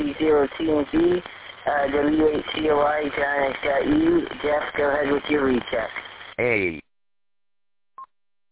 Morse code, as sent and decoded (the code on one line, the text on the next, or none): none